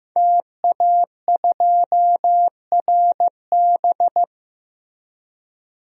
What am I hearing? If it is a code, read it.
Morse code, heard as a keyed tone, about 16 wpm, 700 Hz. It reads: TA2RB
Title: DXCC-100 Call Signs - 5 Characters - Repeated 2x - 15wpm